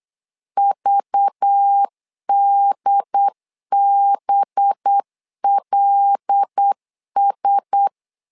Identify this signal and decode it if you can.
Morse code, heard as a keyed tone, about 8 wpm, 800 Hz. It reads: VDBLS
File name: LEZIONI / 5 LBVC / 5 LBV 8wpm 2837